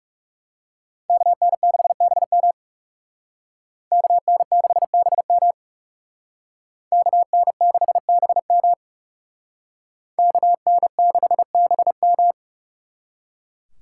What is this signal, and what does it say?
Morse code, heard as a keyed tone, about 36 wpm, 700 Hz. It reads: KN6BM KN6BM KN6BM KN6BM
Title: Speed Racing - USA 2x2 Call Signs 30wpm